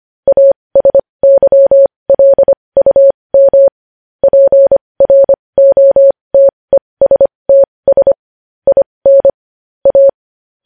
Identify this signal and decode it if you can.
Morse code, heard as a keyed tone, about 25 wpm, spaced as for 20 wpm, 550 Hz. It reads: ASYLUM PROTESTS IN A